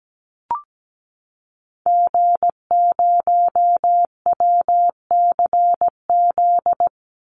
Morse code, heard as a keyed tone, about 17 wpm, 700 Hz. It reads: G0WCZ
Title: Call Signs Repeated 3x - 17wpm